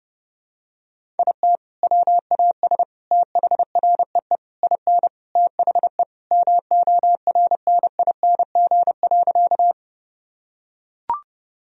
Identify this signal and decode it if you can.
Morse code, heard as a keyed tone, about 30 wpm, 700 Hz. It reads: IT WAS THREE IN THE MORNING.
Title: Sentences from Top 1000 Words - Review Part 1 - 30wpm